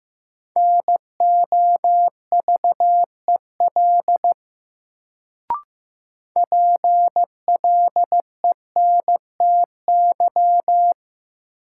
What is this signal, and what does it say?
Morse code, heard as a keyed tone, about 15 wpm, 700 Hz. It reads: NOVEL PLENTY
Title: Sentences from Top 1300 Words - New Word Drill - 15wpm